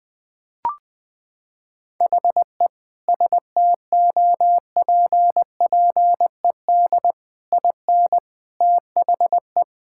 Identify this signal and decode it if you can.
Morse code, heard as a keyed tone, about 20 wpm, 700 Hz. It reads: HE STOPPED IN THE
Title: Sentences with 3-Word Phrases - 20wpm